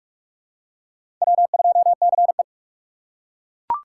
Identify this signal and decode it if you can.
Morse code, heard as a keyed tone, about 42 wpm, 700 Hz. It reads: W2CE
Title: USA Call Signs - 4 Characters - 3x Character Spacing - 45wpm